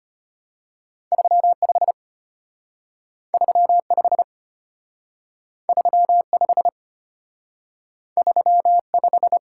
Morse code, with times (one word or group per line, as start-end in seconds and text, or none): 1.1-1.9 35
3.3-4.2 35
5.7-6.7 35
8.2-9.4 35